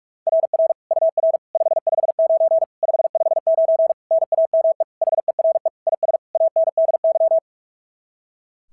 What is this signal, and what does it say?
Morse code, heard as a keyed tone, about 46 wpm, 690 Hz. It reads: RR UR 559 559 NAME HERE IS ANDY
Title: SOTA QSO Practice 45wpm